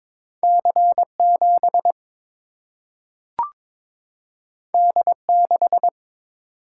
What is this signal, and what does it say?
Morse code, heard as a keyed tone, about 22 wpm, 700 Hz. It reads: C7 D6
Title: Single-Letter Single-Number 22wpm